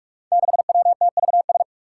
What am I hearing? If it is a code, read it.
Morse code, heard as a keyed tone, about 44 wpm, 700 Hz. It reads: 6WTVS
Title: Five Challenging Letter-Number Combinations 45wpm